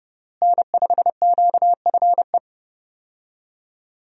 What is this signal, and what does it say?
Morse code, heard as a keyed tone, about 32 wpm, 700 Hz. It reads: N5QFE
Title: USA Call Signs - 5 Characters - Repeated 3x - 30wpm